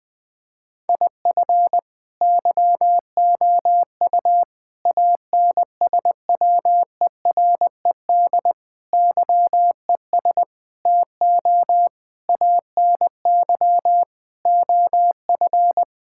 Morse code, heard as a keyed tone, about 20 wpm, 700 Hz. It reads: IF YOU ANSWERED YES TO ANY OF
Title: Sentences with 9-Word Phrases - 20wpm